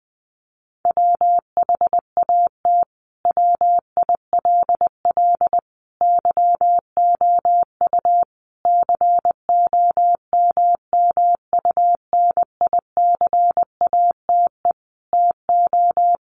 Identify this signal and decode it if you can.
Morse code, heard as a keyed tone, about 20 wpm, 700 Hz. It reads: WHAT WILL YOU COMMUNICATE TO
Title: Sentences from Top 1300 Words 20wpm